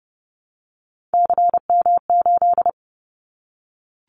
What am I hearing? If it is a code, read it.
Morse code, heard as a keyed tone, about 30 wpm, 700 Hz. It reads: CM8
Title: Callsign Prefixes for Top 100 DXCC - 30wpm